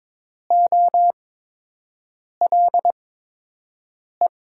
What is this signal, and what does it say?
Morse code, heard as a keyed tone, about 22 wpm, 700 Hz. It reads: O L E
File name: Top 90 QSO Elements - 8x Character Spacing - 22wpm